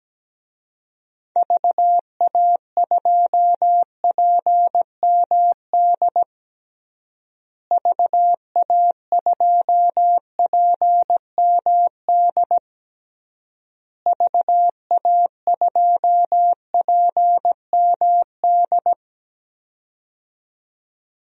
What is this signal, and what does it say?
Morse code, heard as a keyed tone, about 17 wpm, 700 Hz. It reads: VA2PMD VA2PMD VA2PMD
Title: DXCC-100 Call Signs - 6 Characters - Repeated 3x - 17wpm